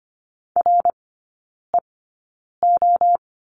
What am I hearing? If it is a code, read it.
Morse code, heard as a keyed tone, about 25 wpm, spaced as for 9 wpm, 700 Hz. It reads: REO
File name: course / 25:10wpm / 093 Lesson 16e - Letter c - Three Characters 25wpm